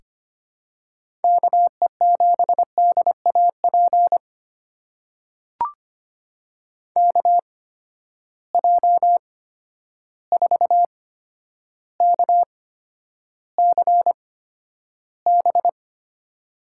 Morse code, heard as a keyed tone, about 25 wpm, spaced as for 11 wpm, 700 Hz. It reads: KE7DAP KJ4KCB
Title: USA Call Signs - 6 Characters - 8x Character Spacing - 25wpm